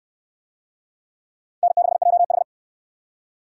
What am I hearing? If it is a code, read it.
Morse code, heard as a keyed tone, about 68 wpm, 700 Hz. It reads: N6JH